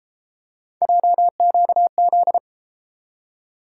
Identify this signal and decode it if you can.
Morse code, heard as a keyed tone, about 31 wpm, 700 Hz. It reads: JQZ